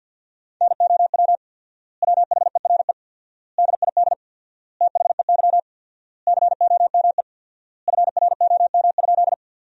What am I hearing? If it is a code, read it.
Morse code, heard as a keyed tone, about 53 wpm, 700 Hz. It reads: NOW WHERE DID THEY COME FROM?
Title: 4x Word-Spacing - Sentences from Top 200 Words - 50wpm